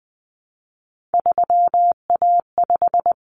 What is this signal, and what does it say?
Morse code, heard as a keyed tone, about 20 wpm, 700 Hz. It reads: 3A5
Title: Common Cut Numbers - 3-Digit 20wpm